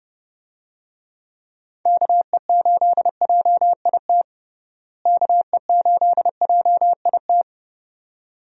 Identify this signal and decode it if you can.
Morse code, heard as a keyed tone, about 30 wpm, 700 Hz. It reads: KE8JIT KE8JIT
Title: USA Call Signs - 6 Characters - Repeated 2x - 30wpm